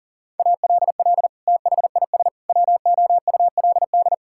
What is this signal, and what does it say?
Morse code, heard as a keyed tone, about 40 wpm, 700 Hz. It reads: ALL THIS WOULD
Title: Sentences from Top 1000 Words - Review Part 6 - 40wpm